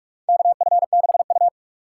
Morse code, heard as a keyed tone, about 46 wpm, 700 Hz. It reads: KF6U
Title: USA Call Signs - 4 Characters - 3x Character Spacing - 45wpm